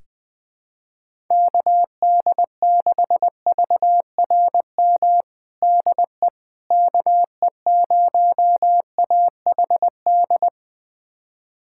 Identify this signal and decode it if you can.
Morse code, heard as a keyed tone, about 20 wpm, 700 Hz. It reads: KD6VRM DE KE0AHD
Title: USA Callsign DE Callsign 20wpm